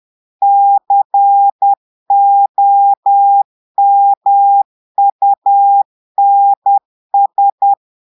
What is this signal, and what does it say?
Morse code, heard as a keyed tone, about 10 wpm, 800 Hz. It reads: COMUNS
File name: portugais texte en clair 2 vitesse 10 mots minutes